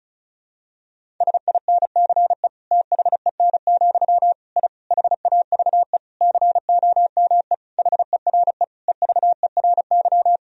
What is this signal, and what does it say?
Morse code, heard as a keyed tone, about 36 wpm, 700 Hz. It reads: SINCE THEN, I HAVE COME HERE EVERY
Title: Sentences from Top 1000 Words - Review Part 1 - 35wpm